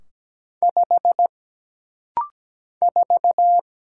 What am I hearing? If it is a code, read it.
Morse code, heard as a keyed tone, about 17 wpm, 700 Hz. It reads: 5 4
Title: Single Letter-Number - Rapid-Fire 17wpm